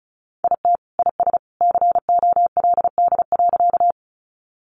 Morse code, heard as a keyed tone, about 34 wpm, 700 Hz. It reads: IT IS COLD.
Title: Sentences from Top 400 Words 35wpm